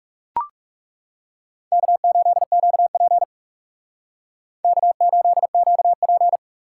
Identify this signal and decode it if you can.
Morse code, heard as a keyed tone, about 40 wpm, 700 Hz. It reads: K8QP K8QP